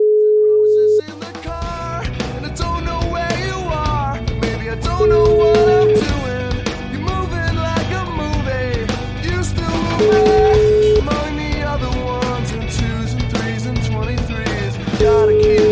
Ringback tone (Europe, 425 Hz), with noise.